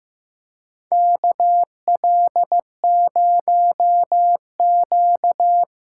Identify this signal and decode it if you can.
Morse code, heard as a keyed tone, about 15 wpm, 700 Hz. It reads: KL0Q
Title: USA Call Signs - 8x Character Spacing - 15wpm